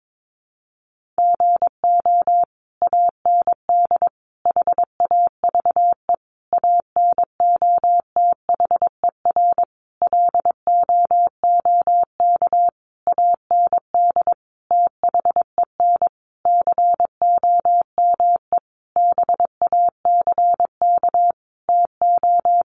Morse code, heard as a keyed tone, about 22 wpm, 700 Hz. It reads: GO AND HAVE ANOTHER LOOK AND THEN COME BACK TO
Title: Sentences from Top 600 Words 22wpm